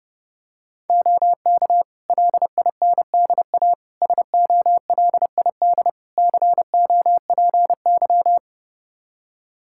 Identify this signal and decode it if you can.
Morse code, heard as a keyed tone, about 30 wpm, 700 Hz. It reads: OK LINDA SOLID COPY
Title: Practice QSO - Exchange 3 - More Info - 30wpm